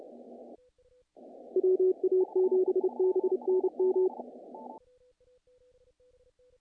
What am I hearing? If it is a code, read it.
Morse code, heard as a keyed tone, about 30 wpm, 370 Hz. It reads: WA7BNM
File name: contests / naqp-cw-k5tr-op-k5pi- -1800